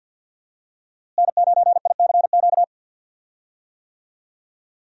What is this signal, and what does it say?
Morse code, heard as a keyed tone, about 47 wpm, 700 Hz. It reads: N9ICQ
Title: USA Call Signs - 5 Characters - Repeated 2x - 50wpm